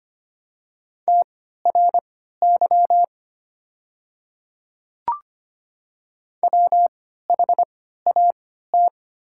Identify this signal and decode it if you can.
Morse code, heard as a keyed tone, about 25 wpm, spaced as for 14 wpm, 690 Hz. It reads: TRY WHAT
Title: 3x Character Spacing - Top 200 Words - 25wpm